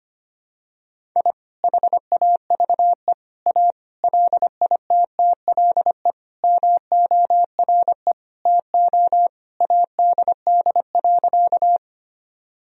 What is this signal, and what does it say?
Morse code, heard as a keyed tone, about 25 wpm, 700 Hz. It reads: I HAVE A LITTLE MORE TO ADD.
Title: Sentences from Top 1000 Words - Review Part 1 - 25wpm